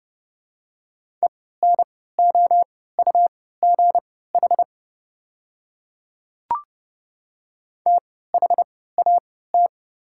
Morse code, heard as a keyed tone, about 30 wpm, spaced as for 17 wpm, 690 Hz. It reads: ENOUGH THAT
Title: 3x Character Spacing - Top 500 Words - 30wpm